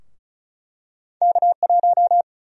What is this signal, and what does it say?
Morse code, heard as a keyed tone, about 36 wpm, 700 Hz. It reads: K1
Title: USA Call Sign Prefixes - 35wpm